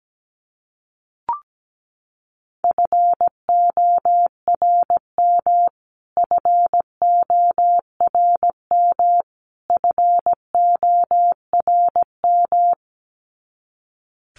Morse code, heard as a keyed tone, about 17 wpm, 700 Hz. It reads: FORM FORM FORM
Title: Top 300 Words Repeated 17wpm